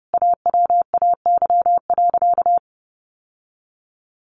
Morse code, heard as a keyed tone, about 28 wpm, 700 Hz. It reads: AWAY.
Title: Sentences from Top 1100 Words 30wpm